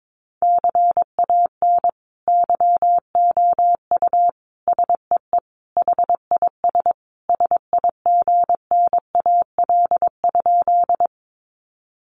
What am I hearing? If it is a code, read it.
Morse code, heard as a keyed tone, about 22 wpm, 700 Hz. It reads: CAN YOU SEE HIS SIGNAL?
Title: Sentences from Top 1400 Words 22wpm